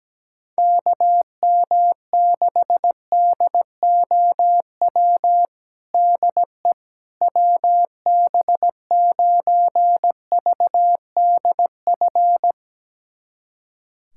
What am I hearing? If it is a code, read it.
Morse code, heard as a keyed tone, about 17 wpm, 700 Hz. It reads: KM6DOW DE WB9VDF
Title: USA Callsign DE Callsign - 6 Characters - 17wpm